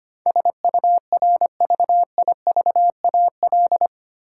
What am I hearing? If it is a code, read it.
Morse code, heard as a keyed tone, about 25 wpm, 700 Hz. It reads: SURVIVAL
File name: practice / 25wpm / 8 Letter Words - First 3 Letters Given - 25wpm